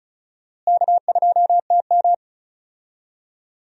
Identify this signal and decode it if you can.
Morse code, heard as a keyed tone, about 36 wpm, 700 Hz. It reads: K2TM
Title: USA Call Signs - 4 Characters - Repeated 2x - 35wpm